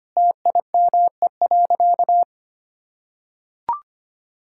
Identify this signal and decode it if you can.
Morse code, heard as a keyed tone, about 25 wpm, 700 Hz. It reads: TIME.
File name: practice / 25wpm / Sentences with 2-Word Phrases - 25wpm